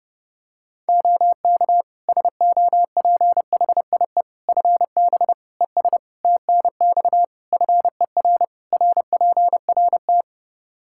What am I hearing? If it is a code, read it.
Morse code, heard as a keyed tone, about 30 wpm, 700 Hz. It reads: OK SOPHIE FB ES TNX FER RPRT
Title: Practice QSO - Exchange 2 - Info - 30wpm